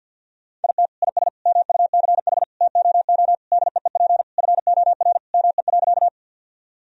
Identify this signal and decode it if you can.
Morse code, heard as a keyed tone, about 48 wpm, 700 Hz. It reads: IT IS MUCH TOO DEEP FOR ME.
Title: Sentences from Top 1000 Words - Review Part 3 - 50wpm